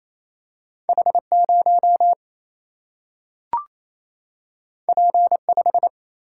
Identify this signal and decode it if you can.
Morse code, heard as a keyed tone, about 28 wpm, 700 Hz. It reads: H0 P5
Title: Single-Letter Single-Number 28wpm